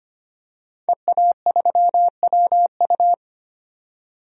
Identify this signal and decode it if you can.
Morse code, heard as a keyed tone, about 25 wpm, 700 Hz. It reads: EA3WU
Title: DXCC-100 Call Signs - 5 Characters - 25wpm